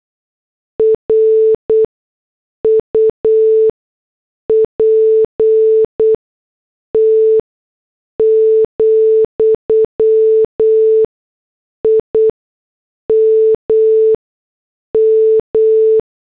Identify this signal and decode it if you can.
Morse code, heard as a keyed tone, about 8 wpm, spaced as for 6 wpm, 440 Hz. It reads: RUPT, IMM
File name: morse / headlines